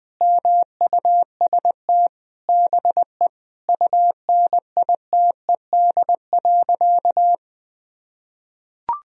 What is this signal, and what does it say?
Morse code, heard as a keyed tone, about 20 wpm, 700 Hz. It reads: MUST BE UNITED.
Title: Sentences from Top 1000 Words - Review Part 4 - 20wpm